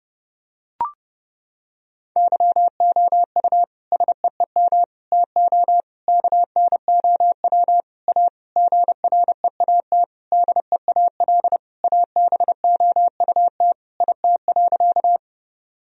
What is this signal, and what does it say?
Morse code, heard as a keyed tone, about 30 wpm, 700 Hz. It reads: YOU SEEM TO KNOW A GREAT DEAL ABOUT IT.
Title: Sentences from Top 1000 Words - Review Part 1 - 30wpm